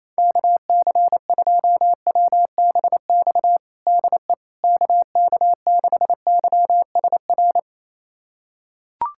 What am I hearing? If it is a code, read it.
Morse code, heard as a keyed tone, about 28 wpm, 700 Hz. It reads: KC2WBX DE KK6YSR
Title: USA Callsign DE Callsign - 6 Characters - 28wpm